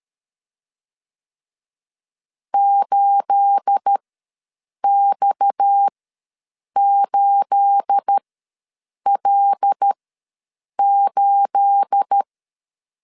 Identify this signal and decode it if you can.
Morse code, heard as a keyed tone, about 13 wpm, spaced as for 7 wpm, 800 Hz. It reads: 8X8L8
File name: LEZIONI / 9 28 ws 12wpm 3354